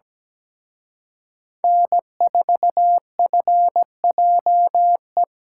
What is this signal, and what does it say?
Morse code, heard as a keyed tone, about 21 wpm, 700 Hz. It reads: N4FJE
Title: Speed Racing - USA 1x3 Call Signs 17wpm